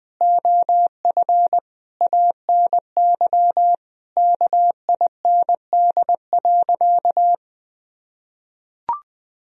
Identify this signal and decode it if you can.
Morse code, heard as a keyed tone, about 20 wpm, 700 Hz. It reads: OF ANY KIND.
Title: Sentences from Top 1000 Words - Review Part 2 - 20wpm